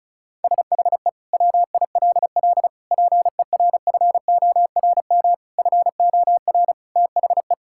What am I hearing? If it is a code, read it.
Morse code, heard as a keyed tone, about 34 wpm, 700 Hz. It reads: SHE WILL PERFORM FOR THE
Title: Sentences from Top 1000 Words - Review Part 5 - 35wpm